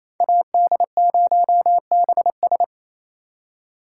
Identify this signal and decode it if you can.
Morse code, heard as a keyed tone, about 36 wpm, 700 Hz. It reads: AD0BS